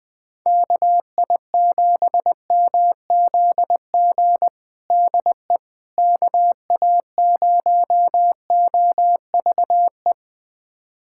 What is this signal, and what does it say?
Morse code, heard as a keyed tone, about 20 wpm, 690 Hz. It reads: KI7MZG DE KA0OVE
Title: USA Callsign DE Callsign 20wpm